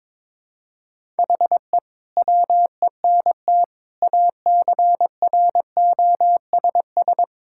Morse code, heard as a keyed tone, about 22 wpm, 700 Hz. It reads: HE WENT ACROSS
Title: Sentences from Top 1000 Words - Review Part 5 - 22wpm